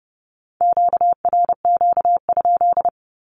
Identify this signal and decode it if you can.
Morse code, heard as a keyed tone, about 30 wpm, 700 Hz. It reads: QRQ?
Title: Q-Codes with Spoken Meaning - 30wpm